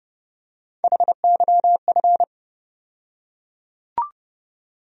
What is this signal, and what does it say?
Morse code, heard as a keyed tone, about 30 wpm, 700 Hz. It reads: HYF
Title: USA Call Sign Postfixes - 3 Letters 30wpm